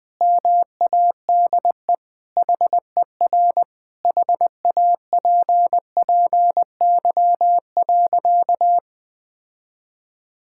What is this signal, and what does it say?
Morse code, heard as a keyed tone, about 20 wpm, 700 Hz. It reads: MADE HER HAPPY.
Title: Sentences with 4-Word Phrases - 20wpm